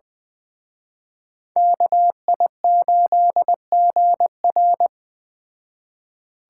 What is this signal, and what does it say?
Morse code, heard as a keyed tone, about 20 wpm, 700 Hz. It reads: KI8GR
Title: Speed Racing - USA 2x2 Call Signs 17wpm